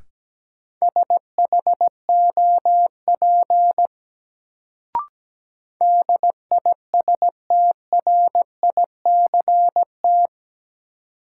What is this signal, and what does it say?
Morse code, heard as a keyed tone, about 17 wpm, 700 Hz. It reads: SHOP DISTRICT
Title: Sentences from Top 1600 Words - New Word Drill - 17wpm